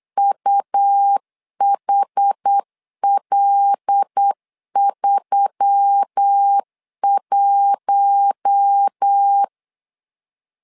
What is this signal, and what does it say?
Morse code, heard as a keyed tone, about 8 wpm, 800 Hz. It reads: UHL31
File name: LEZIONI / 5 LBVC / 5 L 8wpm 2831